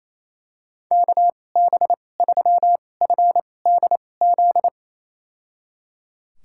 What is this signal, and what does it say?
Morse code, heard as a keyed tone, about 28 wpm, spaced as for 20 wpm, 700 Hz. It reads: KB3FDZ
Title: USA Call Signs - 6 Characters - 2x Character Spacing - 28wpm